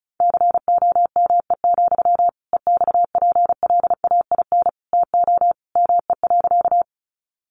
Morse code, heard as a keyed tone, about 34 wpm, 700 Hz. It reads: COME, EXPLAIN TO ME.
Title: Sentences from Top 1100 Words 35wpm